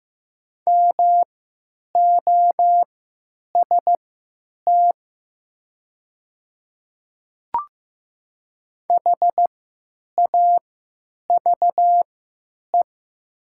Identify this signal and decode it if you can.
Morse code, heard as a keyed tone, about 15 wpm, spaced as for 9 wpm, 700 Hz. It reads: MOST HAVE
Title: 3x Character Spacing - Top 100 Words - 15wpm